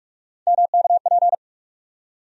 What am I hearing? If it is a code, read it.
Morse code, heard as a keyed tone, about 41 wpm, 700 Hz. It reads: MKP
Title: USA Call Sign Postfixes - 3 Letters 45wpm